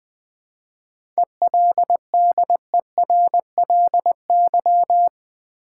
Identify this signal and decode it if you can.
Morse code, heard as a keyed tone, about 20 wpm, 700 Hz. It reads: ELDERLY
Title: 7 Letter Words - First 3 Letters Given - 20wpm